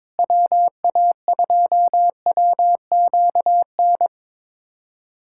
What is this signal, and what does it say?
Morse code, heard as a keyed tone, about 22 wpm, 700 Hz. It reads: WA2WQN
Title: USA Callsign DE Callsign 22wpm